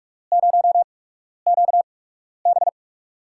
Morse code, heard as a keyed tone, about 45 wpm, spaced as for 12 wpm, 700 Hz. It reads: 0QB
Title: USA Call Signs - 5 Characters - 8x Character Spacing - 45wpm